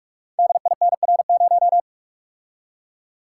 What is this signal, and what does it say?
Morse code, heard as a keyed tone, about 44 wpm, 700 Hz. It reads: DINR0
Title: Code Groups - 5 Letter:Numbers - 45wpm